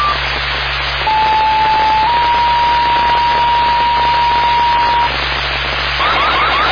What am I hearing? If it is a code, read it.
Two-tone page, heard as a one-second tone, then a three-second tone, about 870 then 950 Hz, with noise. It is heard from 1.1 to 5.1 s.